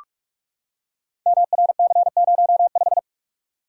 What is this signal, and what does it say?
Morse code, heard as a keyed tone, about 44 wpm, 700 Hz. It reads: MRC05
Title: Code Groups - 5 Letter:Numbers - 45wpm